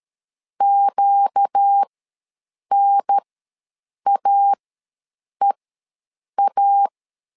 Morse code, heard as a keyed tone, about 13 wpm, spaced as for 7 wpm, 800 Hz. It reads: QNAEA
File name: LEZIONI / 4 Q ws 12wpm 1912